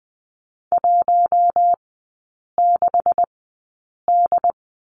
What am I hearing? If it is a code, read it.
Morse code, heard as a keyed tone, about 19 wpm, spaced as for 8 wpm, 700 Hz. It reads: 16D
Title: 184 Lesson 30e - Letter b - Three Characters 20wpm